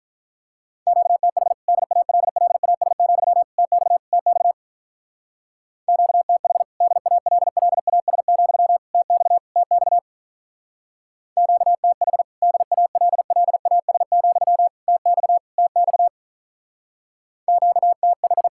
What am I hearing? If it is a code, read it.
Morse code, heard as a keyed tone, about 45 wpm, 700 Hz. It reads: QTH DALLAS, TX TX QTH DALLAS, TX TX QTH DALLAS, TX TX QTH